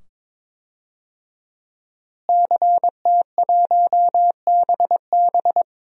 Morse code, heard as a keyed tone, about 22 wpm, 700 Hz. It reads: CT1BB
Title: DXCC-100 Call Signs - 5 Characters - Repeated 2x - 22wpm